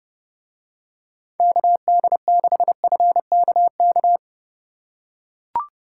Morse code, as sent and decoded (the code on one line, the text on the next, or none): -.- -.. -.... ..-. -.- -.-
KD6FKK